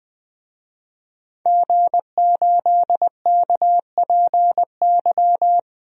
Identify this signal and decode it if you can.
Morse code, heard as a keyed tone, about 20 wpm, 700 Hz. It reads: G8KPY
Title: DXCC-100 Call Signs - 5 Characters - Repeated 2x - 20wpm